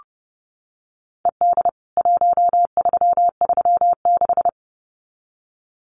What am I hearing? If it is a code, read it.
Morse code, heard as a keyed tone, about 30 wpm, 700 Hz. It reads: ED 1336
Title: CWT Exchange 30wpm